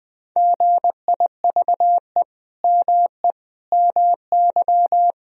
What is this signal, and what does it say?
Morse code, heard as a keyed tone, about 20 wpm, 700 Hz. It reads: GIVE ME MY
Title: Sentences from Top 1000 Words - Review Part 3 - 20wpm